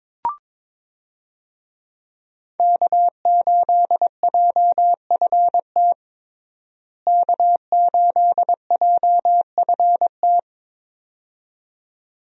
Morse code, heard as a keyed tone, about 22 wpm, 700 Hz. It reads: K8JFT K8JFT
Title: USA Call Signs - 5 Characters - Repeated 2x - 22wpm